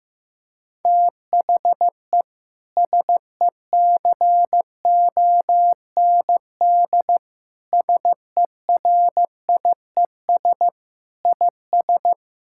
Morse code, heard as a keyed tone, about 16 wpm, 700 Hz. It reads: THE SECOND SERIES IS
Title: Sentences from Top 1000 Words - Review Part 5 - 15wpm